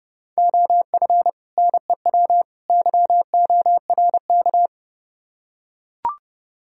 Morse code, heard as a keyed tone, about 32 wpm, 700 Hz. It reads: OF NEW YORK